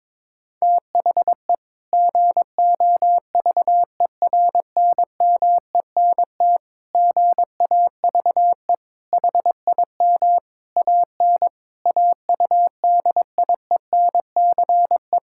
Morse code, heard as a keyed tone, about 21 wpm, 700 Hz. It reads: THE GOVERNMENT GAVE HIM AN AUDIENCE
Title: Sentences from Top 1000 Words - Review Part 5 - 22wpm